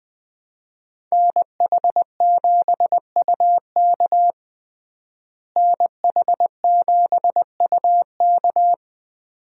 Morse code, heard as a keyed tone, about 20 wpm, 700 Hz. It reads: NH7UK NH7UK
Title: USA Call Signs - 5 Characters - Repeated 2x - 20wpm